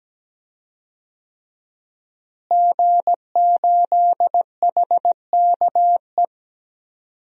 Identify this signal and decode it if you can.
Morse code, heard as a keyed tone, about 17 wpm, 700 Hz. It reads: G8HKE
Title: DXCC-100 Call Signs - 5 Characters - Repeated 3x - 17wpm